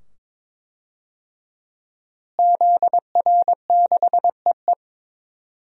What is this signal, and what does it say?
Morse code, heard as a keyed tone, about 22 wpm, 700 Hz. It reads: ZR6EE
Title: DXCC-100 Call Signs - 5 Characters - Repeated 2x - 22wpm